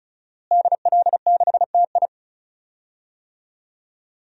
Morse code, heard as a keyed tone, about 34 wpm, 700 Hz. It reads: DL6TI
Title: DXCC-100 Call Signs - 5 Characters - Repeated 2x - 35wpm